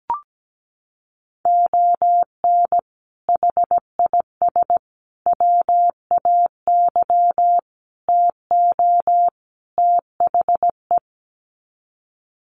Morse code, heard as a keyed tone, about 17 wpm, 700 Hz. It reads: ON HIS WAY TO THE
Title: Sets of 5 Words Encore2 17wpm